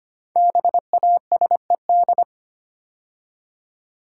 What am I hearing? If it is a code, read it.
Morse code, heard as a keyed tone, about 25 wpm, 700 Hz. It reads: BASED
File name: 5 Letter Words - First 3 Letters Given - 25wpm